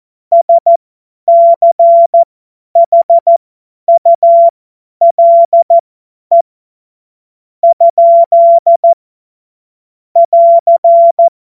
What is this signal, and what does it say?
Morse code, heard as a keyed tone, about 14 wpm, spaced as for 9 wpm, 680 Hz. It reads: SCHULE? <AR>
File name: Morsekurs / Witze / Witz 21. September 2021, 50 F70